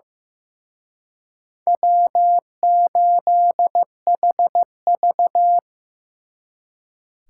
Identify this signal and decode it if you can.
Morse code, heard as a keyed tone, about 18 wpm, 700 Hz. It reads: W8HV
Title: Speed Racing - USA Call Signs 15wpm